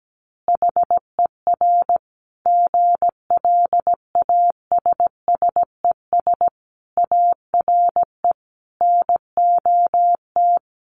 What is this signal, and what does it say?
Morse code, heard as a keyed tone, about 17 wpm, 700 Hz. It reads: HER GLASSES ARE NOT
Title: Additional Easy Phrases 17wpm